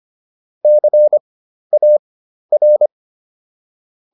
Morse code, heard as a keyed tone, about 25 wpm, spaced as for 12 wpm, 600 Hz. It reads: CAR